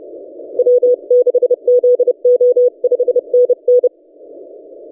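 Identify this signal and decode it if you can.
Morse code, heard as a keyed tone, about 32 wpm, 490 Hz. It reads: W6ZO 5NN
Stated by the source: DX station 3D2GC on 20 metres CW, recorded by N6ML.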